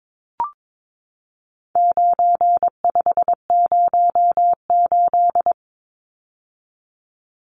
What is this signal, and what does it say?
Morse code, heard as a keyed tone, about 22 wpm, 700 Hz. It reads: 9508
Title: Numbers - 4-Digit 22wpm